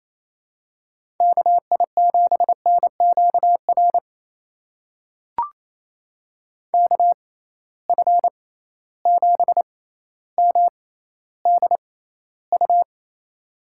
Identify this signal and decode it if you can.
Morse code, heard as a keyed tone, about 28 wpm, spaced as for 14 wpm, 700 Hz. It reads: KI7NQR KF7MDU
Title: USA Call Signs - 6 Characters - 6x Character Spacing - 28wpm